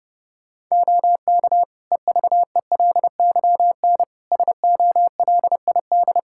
Morse code, heard as a keyed tone, about 28 wpm, 700 Hz. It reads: OK EVELYN SOLID
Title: Practice QSO - Exchange 3 - More Info - 30wpm